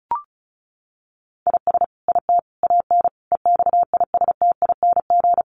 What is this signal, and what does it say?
Morse code, heard as a keyed tone, about 34 wpm, 700 Hz. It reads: IS IT AN EXISTING
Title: Sentences from Top 1500 Words 35wpm